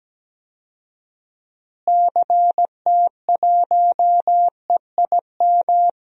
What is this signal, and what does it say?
Morse code, heard as a keyed tone, about 17 wpm, 700 Hz. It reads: CT1EIM
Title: DXCC-100 Call Signs - 6 Characters - Repeated 3x - 17wpm